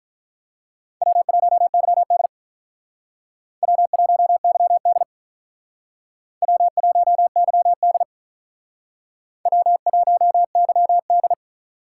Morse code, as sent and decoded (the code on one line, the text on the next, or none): .-- .---- -.-- -.. / .-- .---- -.-- -.. / .-- .---- -.-- -.. / .-- .---- -.-- -..
W1YD W1YD W1YD W1YD